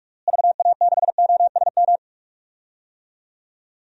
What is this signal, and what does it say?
Morse code, heard as a keyed tone, about 41 wpm, 700 Hz. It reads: VA6OSM
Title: DXCC-100 Call Signs - 6 Characters - Repeated 2x - 45wpm